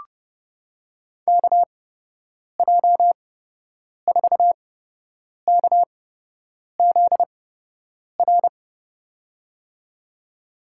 Morse code, heard as a keyed tone, about 30 wpm, spaced as for 8 wpm, 700 Hz. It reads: KJ4KZR
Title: USA Call Signs - 6 Characters - 8x Character Spacing - 30wpm